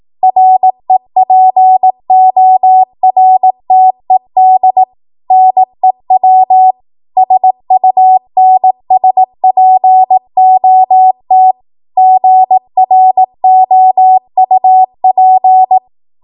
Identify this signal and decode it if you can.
Morse code, heard as a keyed tone, about 18 wpm, 750 Hz. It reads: REPORTED NEW SUNSPOT GROUP
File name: Archive / 18 WPM / 18WPM